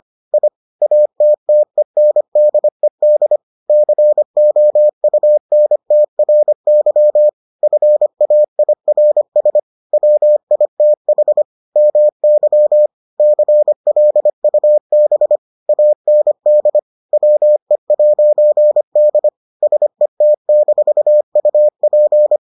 Morse code, heard as a keyed tone, about 25 wpm, 600 Hz. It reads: I ATTENDED COUNTRY FAIRS WITH MY CLUB AND WE'D SET-UP